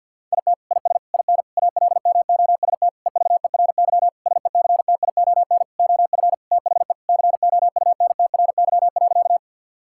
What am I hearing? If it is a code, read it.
Morse code, heard as a keyed tone, about 50 wpm, 700 Hz. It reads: IT IS IN ALMOST EVERY SECTION OF THE COUNTRY.
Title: Sentences from Top 1000 Words - Review Part 5 - 50wpm